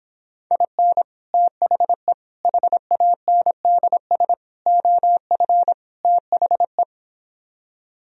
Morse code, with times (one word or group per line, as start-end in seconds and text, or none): 0.5-1.0 IN
1.3-2.1 THE
2.4-4.3 HANDS
4.7-5.7 OF
6.0-6.8 THE